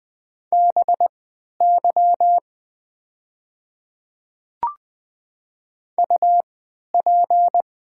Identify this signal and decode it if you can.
Morse code, heard as a keyed tone, about 20 wpm, spaced as for 11 wpm, 690 Hz. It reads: BY UP
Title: 3x Character Spacing - Top 200 Words - 20wpm